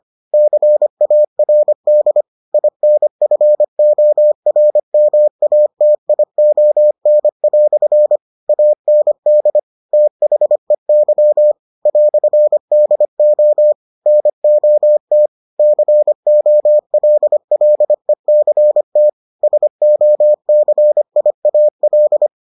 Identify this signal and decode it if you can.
Morse code, heard as a keyed tone, about 25 wpm, 600 Hz. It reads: CARD INFORMATION" AND THEY "DO NOT COLLECT SOCIAL